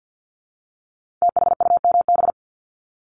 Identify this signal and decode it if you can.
Morse code, heard as a keyed tone, about 50 wpm, 700 Hz. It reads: THUMB
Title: Multi-letter Phonemes Repeated 3x 50wpm